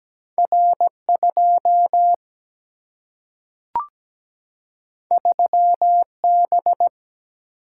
Morse code, heard as a keyed tone, about 17 wpm, 700 Hz. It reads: R 2 3 B
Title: Single-Letter Single-Number 17wpm